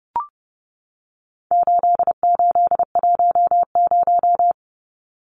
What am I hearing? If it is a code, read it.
Morse code, heard as a keyed tone, about 30 wpm, 700 Hz. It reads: 8810
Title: Numbers - 4-Digit 30wpm